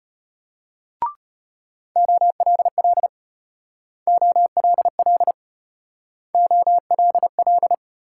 Morse code, heard as a keyed tone, about 31 wpm, 700 Hz. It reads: OLL OLL OLL
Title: Speed-Racing Multi-letter Phonemes - 25wpm